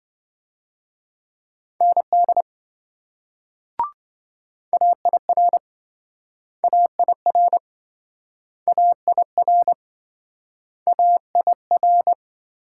Morse code, heard as a keyed tone, about 27 wpm, 700 Hz. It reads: ND AIR AIR AIR AIR